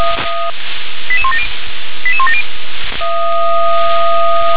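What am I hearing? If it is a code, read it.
Five-tone selective call (ZVEI), heard as quick runs of five tones, tones of about 70 ms. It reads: 88170 88170
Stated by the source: ZVEI five-tone call received by SDR.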